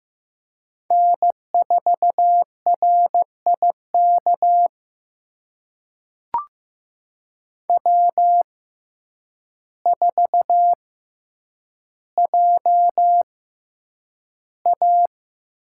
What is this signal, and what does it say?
Morse code, heard as a keyed tone, about 15 wpm, spaced as for 8 wpm, 700 Hz. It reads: N4RIK W4JA
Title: USA Call Signs - 6x Character Spacing - 15wpm